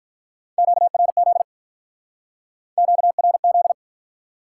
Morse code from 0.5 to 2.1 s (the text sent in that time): QRZ